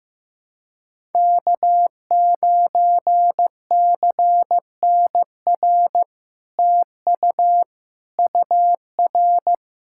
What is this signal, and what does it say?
Morse code, heard as a keyed tone, about 15 wpm, 700 Hz. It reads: K9CNR TU UR
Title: SOTA QSO Practice 15wpm